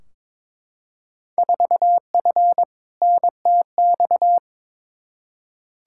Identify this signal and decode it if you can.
Morse code, heard as a keyed tone, about 21 wpm, 700 Hz. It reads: 4F NTX
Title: Field Day Exchanges Encore - 22wpm